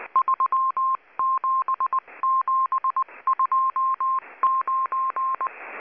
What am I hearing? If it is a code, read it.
Morse code, heard as a keyed tone, about 20 wpm, 1060 Hz. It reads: 37729
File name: HM02- -0540utc